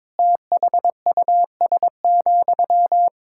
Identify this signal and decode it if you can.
Morse code, heard as a keyed tone, about 22 wpm, 700 Hz. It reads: THUS,
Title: Sentences from Top 1000 Words - Review Part 5 - 22wpm